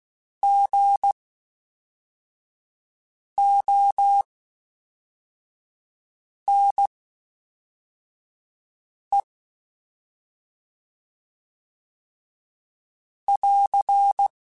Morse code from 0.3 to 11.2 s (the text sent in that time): GONE